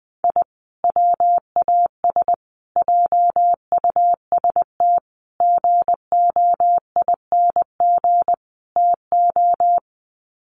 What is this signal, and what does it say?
Morse code, heard as a keyed tone, about 20 wpm, 700 Hz. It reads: I WAS JUST GOING TO
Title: Sets of 5 Words Encore 20wpm